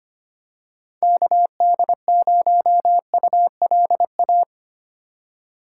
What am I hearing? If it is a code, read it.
Morse code, heard as a keyed tone, about 25 wpm, 700 Hz. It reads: KD0ULA
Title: USA Call Signs - 6 Characters - 4x Character Spacing - 25wpm